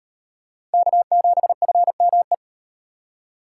Morse code, heard as a keyed tone, about 35 wpm, 700 Hz. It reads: K7FME